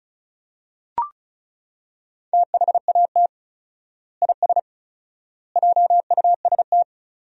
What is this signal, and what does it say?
Morse code, heard as a keyed tone, about 34 wpm, 700 Hz. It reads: THAT IS JUST
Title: 4x Word-Spacing - Sentences from Top 200 Words - 35wpm